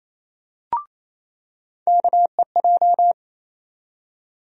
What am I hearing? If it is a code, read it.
Morse code, heard as a keyed tone, about 28 wpm, 700 Hz. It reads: KEJ
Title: USA Call Sign Postfixes - 3 Letters 28wpm